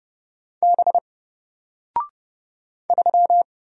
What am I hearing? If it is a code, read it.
Morse code, heard as a keyed tone, about 30 wpm, 700 Hz. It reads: B 3
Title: Single Character (Letter:Number:Punctuation:Prosign) Alt 30wpm